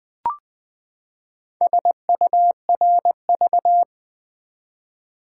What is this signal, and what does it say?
Morse code, heard as a keyed tone, about 20 wpm, 700 Hz. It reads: SURV
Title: Build A Word - 4 Letters At A Time - 12-Letter Words - 20wpm